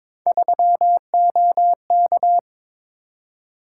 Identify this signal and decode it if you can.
Morse code, heard as a keyed tone, about 22 wpm, 700 Hz. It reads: 3OK
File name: practice / 22wpm / Three Letter-Number Combinations - 22wpm